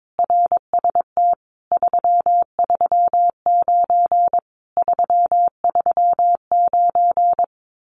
Morse code, heard as a keyed tone, about 22 wpm, 700 Hz. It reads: RST 339 339
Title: Common QSO Phrases Encore 22wpm